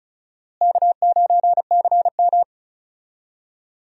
Morse code, heard as a keyed tone, about 34 wpm, 700 Hz. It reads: K9CM
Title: USA Call Signs - 4 Characters - Repeated 2x - 35wpm